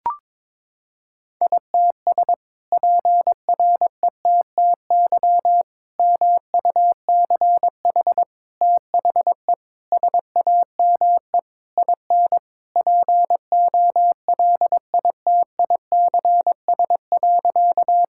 Morse code, heard as a keyed tone, about 22 wpm, 700 Hz. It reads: ITS PRETTY MUCH THE SAME IN POLITICS.
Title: Sentences from Top 1000 Words - Review Part 6 - 22wpm